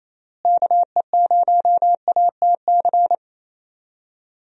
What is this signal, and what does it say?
Morse code, heard as a keyed tone, about 28 wpm, 700 Hz. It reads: KE0ATC
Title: USA Call Signs - 6 Characters - Repeated 3x - 28wpm